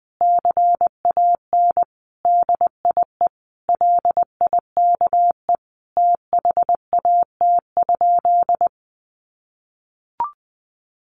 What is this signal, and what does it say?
Morse code, heard as a keyed tone, about 20 wpm, 700 Hz. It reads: CAN DIE LIKE THAT?
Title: Sentences from Top 1400 Words 20wpm